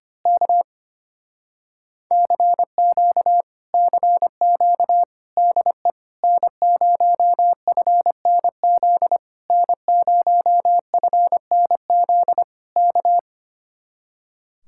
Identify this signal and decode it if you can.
Morse code, heard as a keyed tone, about 24 wpm, 700 Hz. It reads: K CQ CQ DE N0FNZ N0FNZ K
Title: Speed-Racing Common QSO Phrases - 25wpm